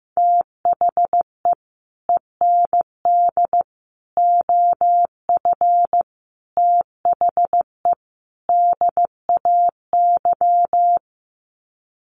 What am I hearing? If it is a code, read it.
Morse code, heard as a keyed tone, about 15 wpm, 700 Hz. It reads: THE END OF THE DAY
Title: Sets of 5 Words 15wpm